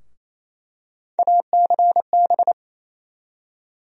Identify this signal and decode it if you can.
Morse code, heard as a keyed tone, about 28 wpm, 700 Hz. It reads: ACB
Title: USA Call Sign Postfixes - 3 Letters 28wpm